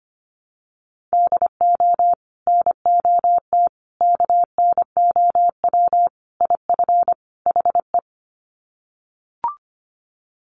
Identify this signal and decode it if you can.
Morse code, heard as a keyed tone, about 25 wpm, 700 Hz. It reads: DO NOT KNOW IF HE
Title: Sets of 5 Words Encore2 25wpm